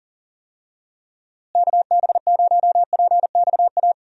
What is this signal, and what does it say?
Morse code, heard as a keyed tone, about 40 wpm, 700 Hz. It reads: KB0PXA
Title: USA Call Signs - 6 Characters - Repeated 3x - 40wpm